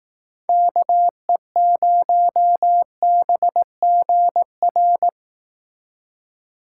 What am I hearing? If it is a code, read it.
Morse code, heard as a keyed tone, about 18 wpm, 700 Hz. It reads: KE0BGR
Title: Speed Racing - USA 2x3 Call Signs 15wpm